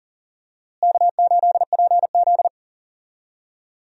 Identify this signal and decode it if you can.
Morse code, heard as a keyed tone, about 40 wpm, 700 Hz. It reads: K8PZ
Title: USA Call Signs - 4 Characters - Repeated 3x - 40wpm